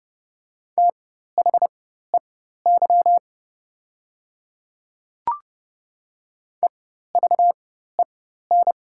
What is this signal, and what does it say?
Morse code, heard as a keyed tone, about 30 wpm, spaced as for 14 wpm, 700 Hz. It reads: THEY EVEN
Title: 4x Character Spacing - Top 200 Words - 30wpm